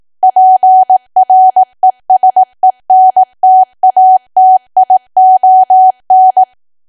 Morse code, heard as a keyed tone, about 18 wpm, 750 Hz. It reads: PRESENTATION
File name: Archive / 18 WPM / 18WPM